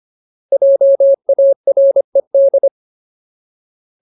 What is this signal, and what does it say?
Morse code, heard as a keyed tone, about 25 wpm, 550 Hz. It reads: JARED
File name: Ultimate Word List / 25WPM Ultimate